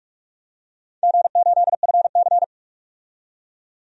Morse code, heard as a keyed tone, about 46 wpm, 700 Hz. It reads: G8FC
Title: DXCC-100 Call Signs - 4 Characters - 45wpm